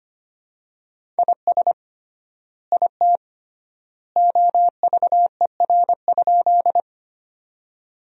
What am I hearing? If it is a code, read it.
Morse code, heard as a keyed tone, about 25 wpm, 700 Hz. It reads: IS IT OVER?
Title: 3x Word-Spacing - Sentences from Top 100 Words - 25wpm